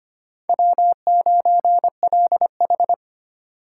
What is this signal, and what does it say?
Morse code, heard as a keyed tone, about 25 wpm, 700 Hz. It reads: W9LH
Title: USA Call Signs - 4 Characters - 4x Character Spacing - 25wpm